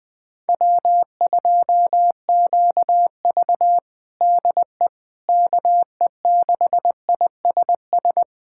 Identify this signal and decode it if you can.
Morse code, heard as a keyed tone, about 20 wpm, 700 Hz. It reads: W2QV DE KE6ISS
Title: USA Callsign DE Callsign 20wpm